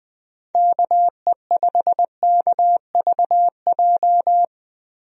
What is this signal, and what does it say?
Morse code, heard as a keyed tone, about 20 wpm, 700 Hz. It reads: KE5KVJ
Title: USA Call Signs - 6 Characters - Repeated 3x - 20wpm